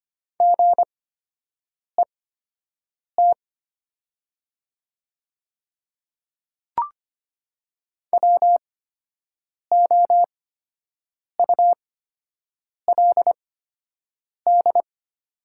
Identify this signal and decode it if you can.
Morse code, heard as a keyed tone, about 25 wpm, spaced as for 7 wpm, 700 Hz. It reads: GET WOULD
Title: 8x Character Spacing - Top 300 Words - 25wpm